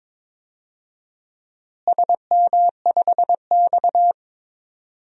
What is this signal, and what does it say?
Morse code, heard as a keyed tone, about 22 wpm, 700 Hz. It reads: SM5X
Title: DXCC-100 Call Signs - 4 Characters - Repeated 2x - 22wpm